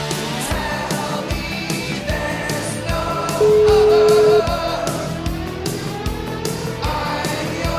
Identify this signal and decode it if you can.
Ringback tone (Europe, 425 Hz), with noise.